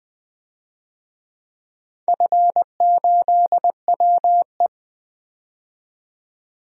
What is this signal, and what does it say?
Morse code, heard as a keyed tone, about 20 wpm, 700 Hz. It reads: F8WE
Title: DXCC-100 Call Signs - 4 Characters - 20wpm